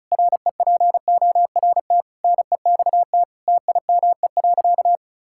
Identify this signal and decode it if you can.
Morse code, heard as a keyed tone, about 36 wpm, 700 Hz. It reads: REPORT NEXT TIME.
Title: Sentences from Top 1000 Words - Review Part 6 - 35wpm